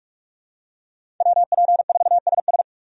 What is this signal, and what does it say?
Morse code, heard as a keyed tone, about 44 wpm, 700 Hz. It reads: WP4SS
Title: USA Call Signs - 5 Characters - 45wpm